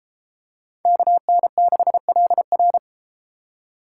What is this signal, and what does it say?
Morse code, heard as a keyed tone, about 31 wpm, 700 Hz. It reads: KN6LR